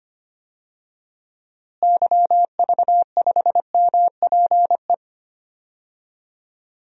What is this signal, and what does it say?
Morse code, heard as a keyed tone, about 25 wpm, 700 Hz. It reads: YV5MPE
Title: DXCC-100 Call Signs - 6 Characters - Repeated 3x - 25wpm